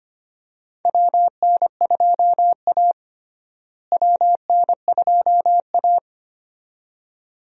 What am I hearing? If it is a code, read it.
Morse code, heard as a keyed tone, about 25 wpm, 700 Hz. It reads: WN2A WN2A
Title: USA Call Signs - 4 Characters - Repeated 2x - 25wpm